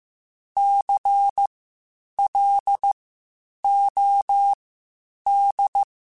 Morse code, heard as a keyed tone, about 15 wpm, spaced as for 8 wpm, 800 Hz. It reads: CLOD
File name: Spaced Alphas / Spaced Alphas mp3s / Cs sp @13